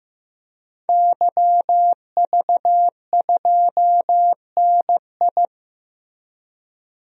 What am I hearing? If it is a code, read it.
Morse code, heard as a keyed tone, about 15 wpm, 700 Hz. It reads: YV2NI
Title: DXCC-100 Call Signs - 5 Characters - Repeated 2x - 15wpm